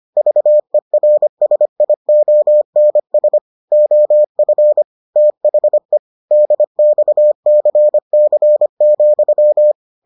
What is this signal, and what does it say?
Morse code, heard as a keyed tone, about 25 wpm, 600 Hz. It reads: VERSIONS OF THE DXCC,